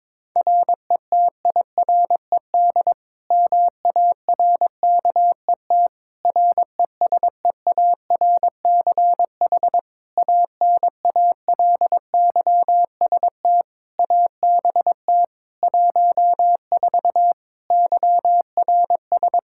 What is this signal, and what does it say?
Morse code, heard as a keyed tone, about 22 wpm, 700 Hz. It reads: RETIRED MARKET RESEARCH ANALYST ABT 14 YRS
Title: Practice QSO - Exchange 3 - More Info - 22wpm